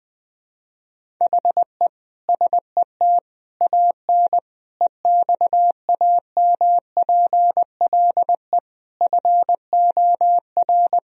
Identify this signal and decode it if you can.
Morse code, heard as a keyed tone, about 20 wpm, 700 Hz. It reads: HE SET AN EXAMPLE FOR
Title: Sentences from Top 1000 Words - Review Part 1 - 20wpm